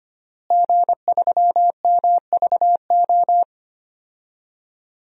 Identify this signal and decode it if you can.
Morse code, heard as a keyed tone, about 25 wpm, 700 Hz. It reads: G3MVO
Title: DXCC-100 Call Signs - 5 Characters - Repeated 2x - 25wpm